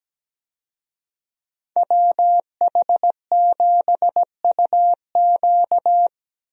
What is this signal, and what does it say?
Morse code, heard as a keyed tone, about 17 wpm, 700 Hz. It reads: WH7UQ
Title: USA Call Signs - 5 Characters - 17wpm